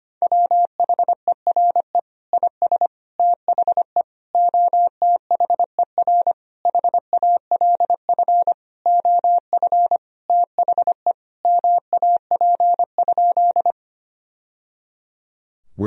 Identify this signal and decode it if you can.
Morse code, heard as a keyed tone, about 25 wpm, 700 Hz. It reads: WHERE IS THE OTHER HALF OF THE MAP?
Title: Sentences from Top 1000 Words - Review Part 2 - 25wpm